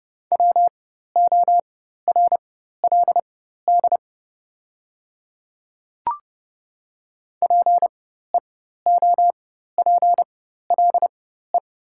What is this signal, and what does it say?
Morse code, heard as a keyed tone, about 30 wpm, spaced as for 14 wpm, 700 Hz. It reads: WORLD PEOPLE
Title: 4x Character Spacing - Top 200 Words - 30wpm